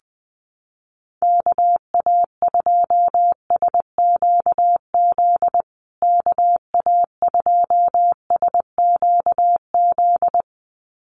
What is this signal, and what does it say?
Morse code, heard as a keyed tone, about 19 wpm, 700 Hz. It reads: KA2SQZ KA2SQZ
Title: USA Call Signs Repeated 2x 20wpm